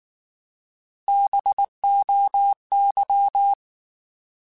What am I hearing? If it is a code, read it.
Morse code, heard as a keyed tone, about 19 wpm, 800 Hz. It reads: BOY
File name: Alphabetical / Alphabetical mp3s / Bs @18